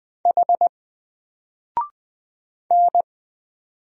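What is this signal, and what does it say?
Morse code, heard as a keyed tone, about 20 wpm, 700 Hz. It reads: H N
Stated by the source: Single letter and number Morse drill at 20 wpm, from N4LCD.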